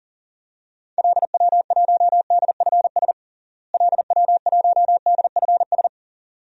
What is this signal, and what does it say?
Morse code, heard as a keyed tone, about 40 wpm, 700 Hz. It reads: LW1DFS LW1DFS
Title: DXCC-100 Call Signs - 6 Characters - Repeated 2x - 40wpm